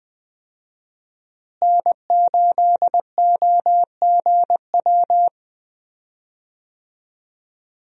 Morse code, heard as a keyed tone, about 20 wpm, 700 Hz. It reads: N8OGW
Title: USA Call Signs - 5 Characters - Repeated 2x - 20wpm